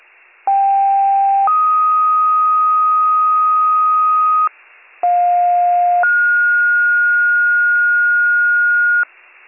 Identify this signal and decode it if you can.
Two-tone page, heard as a one-second tone, then a three-second tone, about 790 then 1250 Hz, about 710 then 1430 Hz, with noise. It is heard from 0.5 to 9.0 s.